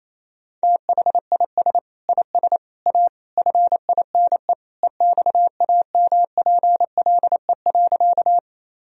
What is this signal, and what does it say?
Morse code, heard as a keyed tone, about 28 wpm, 700 Hz. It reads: THIS IS A FINE EXAMPLE.
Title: Sentences from Top 1000 Words - Review Part 1 - 28wpm